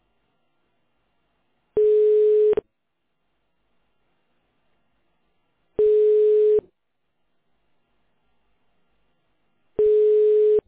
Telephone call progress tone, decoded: ringback tone

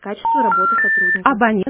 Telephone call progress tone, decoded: special information tone

